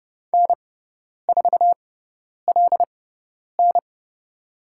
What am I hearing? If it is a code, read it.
Morse code, heard as a keyed tone, about 30 wpm, spaced as for 10 wpm, 700 Hz. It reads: N4LN
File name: Lesson 17g - Letter u - Call Signs 30wpm